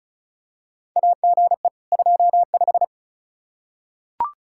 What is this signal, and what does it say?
Morse code, heard as a keyed tone, about 36 wpm, 700 Hz. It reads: AGE 25
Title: Practice QSO - Exchange 3 - More Info - 35wpm